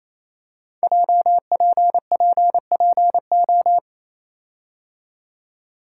Morse code, heard as a keyed tone, about 28 wpm, 700 Hz. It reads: JPPPO
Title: Code Groups - 5 Letters - 28wpm